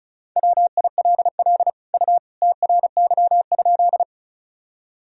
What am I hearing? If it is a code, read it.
Morse code, heard as a keyed tone, about 36 wpm, 700 Hz. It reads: WILL U TRY?
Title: Sentences from Top 200 Words and Abbreviations 35wpm